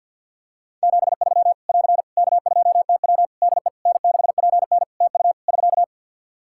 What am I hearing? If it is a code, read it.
Morse code, heard as a keyed tone, about 52 wpm, 700 Hz. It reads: 73 <AR> K2TW DE N6PN TU <SK>
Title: Practice QSO - Exchange 4 - Ending - 50wpm